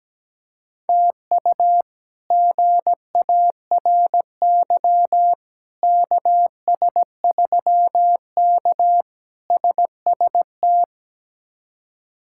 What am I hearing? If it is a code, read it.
Morse code, heard as a keyed tone, about 17 wpm, 700 Hz. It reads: TU GARY KS3K SST
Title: K1USN SST QSO Practice - 17wpm